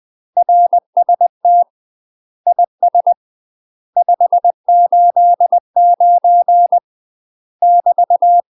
Morse code, heard as a keyed tone, about 20 wpm, 700 Hz. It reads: RST IS 589 <BT>
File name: UK MASTER QSOs TRAINING / 20WPM / UK MASTER QSOs 0048